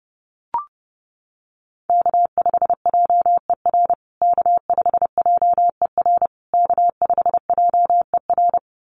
Morse code, heard as a keyed tone, about 30 wpm, 700 Hz. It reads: K5JER K5JER K5JER
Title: USA Call Signs Repeated 3x 30wpm